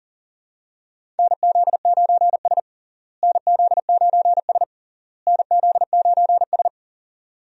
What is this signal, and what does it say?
Morse code, heard as a keyed tone, about 40 wpm, 700 Hz. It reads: NZ9S NZ9S NZ9S
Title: USA Call Signs - 4 Characters - Repeated 3x - 40wpm